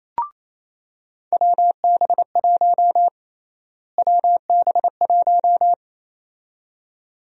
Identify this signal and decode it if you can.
Morse code, heard as a keyed tone, about 28 wpm, 700 Hz. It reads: WB1 WB1
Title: USA Call Sign Prefixes - Repeated 2x - 28wpm